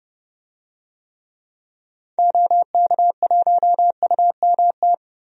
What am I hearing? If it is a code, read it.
Morse code, heard as a keyed tone, about 30 wpm, 700 Hz. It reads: OK1UMT
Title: DXCC-100 Call Signs - 6 Characters - Repeated 3x - 30wpm